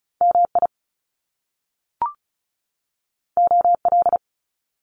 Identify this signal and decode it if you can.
Morse code, heard as a keyed tone, about 36 wpm, 700 Hz. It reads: MI OL
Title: Top Two Letter Combinations in Words 35wpm